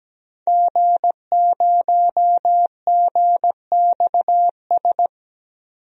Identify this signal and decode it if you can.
Morse code, heard as a keyed tone, about 17 wpm, 700 Hz. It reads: G0GXS
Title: DXCC-100 Call Signs - 5 Characters - 17wpm